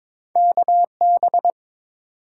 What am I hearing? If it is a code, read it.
Morse code, heard as a keyed tone, about 22 wpm, 700 Hz. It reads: KB
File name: practice / 22wpm / Two Letter-Number Combinations - 22wpm